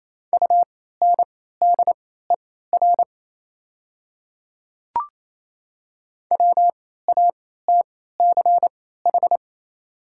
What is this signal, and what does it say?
Morse code, heard as a keyed tone, about 28 wpm, spaced as for 16 wpm, 690 Hz. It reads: UNDER WATCH
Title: 3x Character Spacing - Top 300 Words - 28wpm